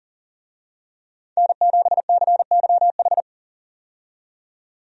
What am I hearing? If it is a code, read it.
Morse code, heard as a keyed tone, about 40 wpm, 700 Hz. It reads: N7CYH
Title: USA Call Signs - 5 Characters - 40wpm